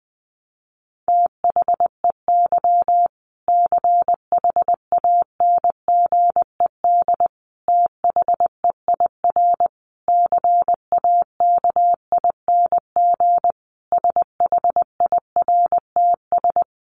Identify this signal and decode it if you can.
Morse code, heard as a keyed tone, about 19 wpm, 700 Hz. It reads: THEY CHANGED THEIR CAKING SHIRTS
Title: A Sound Of Thunder 20wpm